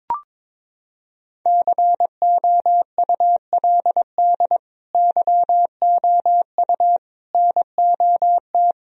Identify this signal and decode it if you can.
Morse code, heard as a keyed tone, about 22 wpm, 700 Hz. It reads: COULD YOU NOT
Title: Sentences from Top 1000 Words - Review Part 1 - 22wpm